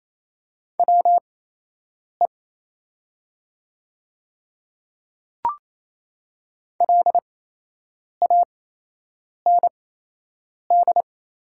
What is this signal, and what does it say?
Morse code, heard as a keyed tone, about 27 wpm, spaced as for 8 wpm, 700 Hz. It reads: WE LAND
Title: 8x Character Spacing - Top 200 Words - 28wpm